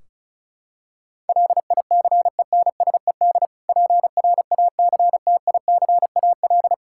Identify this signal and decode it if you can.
Morse code, heard as a keyed tone, about 36 wpm, 700 Hz. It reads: LICENSED PRACTICAL
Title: Practice QSO - Exchange 3 - More Info - 35wpm